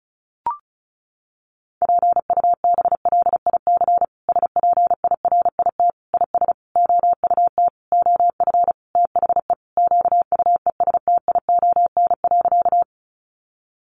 Morse code, heard as a keyed tone, about 34 wpm, 700 Hz. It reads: PUBLIC SPIRIT IS OUT OF THE QUESTION.
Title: Sentences from Top 1400 Words 35wpm